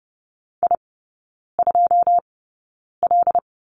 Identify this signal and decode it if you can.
Morse code, heard as a keyed tone, about 30 wpm, spaced as for 9 wpm, 700 Hz. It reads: I2L
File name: Lesson 17e - Letter u - Three Characters 30wpm